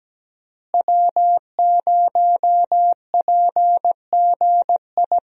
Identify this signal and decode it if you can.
Morse code, heard as a keyed tone, about 17 wpm, 700 Hz. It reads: W0PGI
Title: USA Call Signs - 5 Characters - Repeated 3x - 17wpm